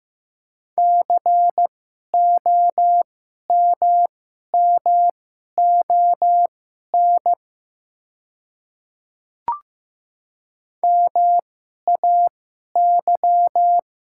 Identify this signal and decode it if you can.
Morse code, heard as a keyed tone, about 15 wpm, spaced as for 11 wpm, 700 Hz. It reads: COMMON MAY
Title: 2x Character Spacing - Top 500 Words - 15wpm